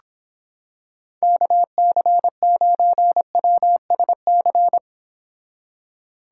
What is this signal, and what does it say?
Morse code, heard as a keyed tone, about 26 wpm, 700 Hz. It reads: KC9WSC